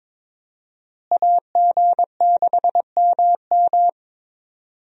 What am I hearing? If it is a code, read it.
Morse code, heard as a keyed tone, about 21 wpm, 700 Hz. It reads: AG6MM
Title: USA Call Signs - 5 Characters - Repeated 3x - 22wpm